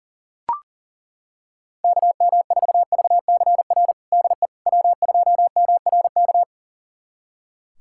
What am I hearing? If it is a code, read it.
Morse code, heard as a keyed tone, about 40 wpm, 700 Hz. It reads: KM4VCR DE W2MRK
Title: USA Callsign DE Callsign 40wpm